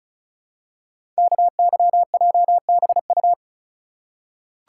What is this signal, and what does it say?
Morse code, heard as a keyed tone, about 36 wpm, 690 Hz. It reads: KYJBU
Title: Code Groups - 5 Letters - 35wpm